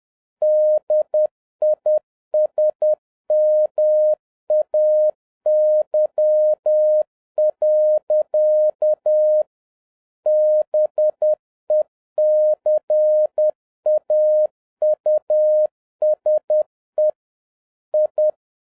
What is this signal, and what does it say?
Morse code, heard as a keyed tone, about 10 wpm, 610 Hz. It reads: DISMAY. BECAUSE I
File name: 2011-07-mp3 / n56 BACK FROM A BIG ACCIDENT 10wpm